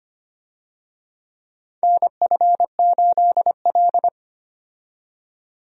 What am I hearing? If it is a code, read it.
Morse code, heard as a keyed tone, about 25 wpm, 700 Hz. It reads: NF8L
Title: USA Call Signs - 4 Characters - 25wpm